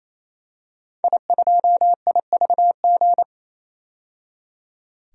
Morse code, heard as a keyed tone, about 27 wpm, 700 Hz. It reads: I2IVG
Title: DXCC-100 Call Signs - 5 Characters - 28wpm